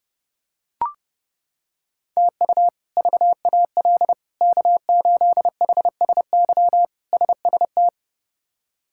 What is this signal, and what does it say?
Morse code, heard as a keyed tone, about 30 wpm, 700 Hz. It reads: TU VAL K8HSY SST
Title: K1USN SST QSO Practice - 30wpm